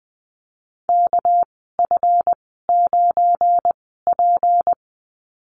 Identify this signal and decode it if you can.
Morse code, heard as a keyed tone, about 20 wpm, spaced as for 15 wpm, 700 Hz. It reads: KF9P
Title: 219 Lesson 35g - Number 0 - Call Signs 20wpm